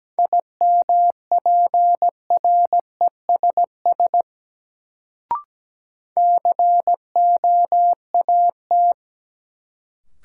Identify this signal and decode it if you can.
Morse code, heard as a keyed tone, about 17 wpm, 700 Hz. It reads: IMPRESS COAT
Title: Sentences from Top 1600 Words - New Word Drill Alt - 17wpm